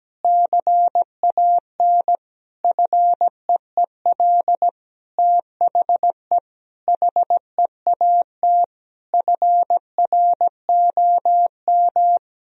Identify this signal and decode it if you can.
Morse code, heard as a keyed tone, about 17 wpm, 700 Hz. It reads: CAN FEEL THE HEAT FROM
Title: Sentences from Top 1000 Words - Review Part 3 - 17wpm